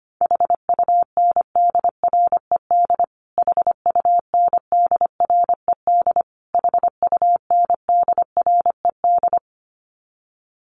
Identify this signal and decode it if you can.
Morse code, heard as a keyed tone, about 25 wpm, 700 Hz. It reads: HUNDRED HUNDRED HUNDRED
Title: Top 500 Words Repeated 25wpm